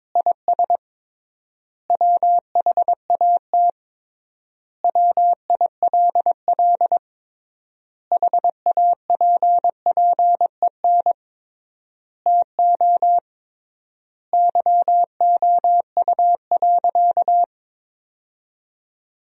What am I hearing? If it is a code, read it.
Morse code, heard as a keyed tone, about 22 wpm, 700 Hz. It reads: IS WHAT WILL HAPPEN TO YOU.
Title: 3x Word-Spacing - Sentences from Top 500 Words - 22wpm